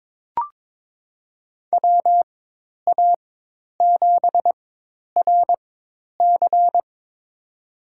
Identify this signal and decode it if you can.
Morse code, heard as a keyed tone, about 23 wpm, spaced as for 10 wpm, 700 Hz. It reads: WA7RC
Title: USA Call Signs - 5 Characters - 4x Character Spacing - 22wpm